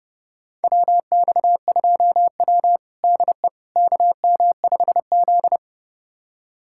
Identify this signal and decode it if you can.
Morse code, heard as a keyed tone, about 30 wpm, 700 Hz. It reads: WX2W DE KM5Z
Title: USA Callsign DE Callsign - 4 Characters - 30wpm